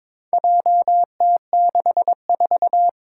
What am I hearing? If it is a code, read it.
Morse code, heard as a keyed tone, about 21 wpm, 700 Hz. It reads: JT64
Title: Four Challenging Letter-Number Combinations 22wpm